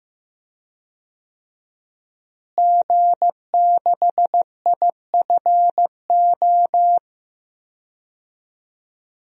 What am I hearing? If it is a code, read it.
Morse code, heard as a keyed tone, about 15 wpm, 700 Hz. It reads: G6IFO
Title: DXCC-100 Call Signs - 5 Characters - Repeated 3x - 15wpm